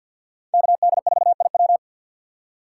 Morse code, heard as a keyed tone, about 48 wpm, 700 Hz. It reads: KD4IW
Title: USA Call Signs - 5 Characters - 3x Character Spacing - 50wpm